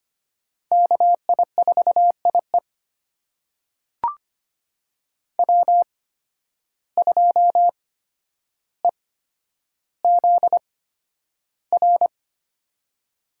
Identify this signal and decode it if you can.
Morse code, heard as a keyed tone, about 25 wpm, spaced as for 11 wpm, 700 Hz. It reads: KI4IE W2EZR
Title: USA Call Signs - 5 Characters - 8x Character Spacing - 25wpm